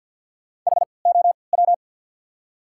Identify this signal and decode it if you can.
Morse code, heard as a keyed tone, about 54 wpm, spaced as for 29 wpm, 700 Hz. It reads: HOW
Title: 3x Character Spacing - Top 100 Words - 50wpm